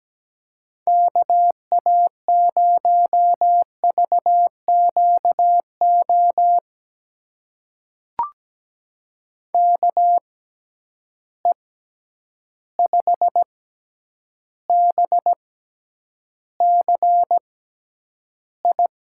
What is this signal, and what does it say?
Morse code, heard as a keyed tone, about 17 wpm, spaced as for 9 wpm, 700 Hz. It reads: KA0VQO KE5BCI
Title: USA Call Signs - 6 Characters - 6x Character Spacing - 17wpm